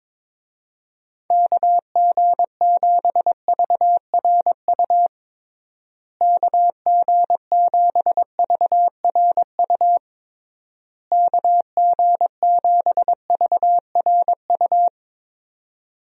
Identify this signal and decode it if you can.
Morse code, heard as a keyed tone, about 22 wpm, 700 Hz. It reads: KG7VRU KG7VRU KG7VRU
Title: USA Call Signs - 6 Characters - Repeated 3x - 22wpm